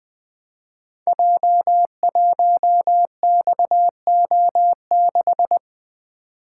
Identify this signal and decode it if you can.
Morse code, heard as a keyed tone, about 20 wpm, 700 Hz. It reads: J1XO6
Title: Code Groups - 5 Letter:Numbers - 20wpm